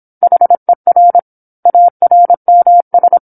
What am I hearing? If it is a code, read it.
Morse code, heard as a keyed tone, about 27 wpm, 700 Hz. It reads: HER ARMS
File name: 08-Chapter-08 26 WPM